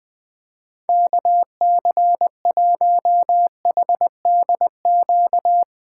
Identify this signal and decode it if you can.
Morse code, heard as a keyed tone, about 20 wpm, 700 Hz. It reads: KC1HDQ
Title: USA Callsign DE Callsign - 6 Characters - 20wpm